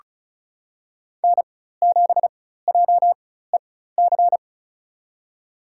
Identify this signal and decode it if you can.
Morse code, heard as a keyed tone, about 34 wpm, spaced as for 16 wpm, 700 Hz. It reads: N7JEC
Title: USA Call Signs - 5 Characters - 4x Character Spacing - 35wpm